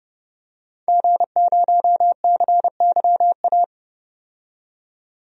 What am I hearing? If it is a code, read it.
Morse code, heard as a keyed tone, about 30 wpm, 700 Hz. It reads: G0CYA
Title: DXCC-100 Call Signs - 5 Characters - 30wpm